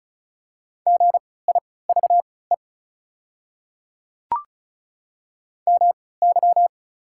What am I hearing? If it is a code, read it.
Morse code, heard as a keyed tone, about 34 wpm, spaced as for 20 wpm, 690 Hz. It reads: GIVE MY
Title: 3x Character Spacing - Top 200 Words - 35wpm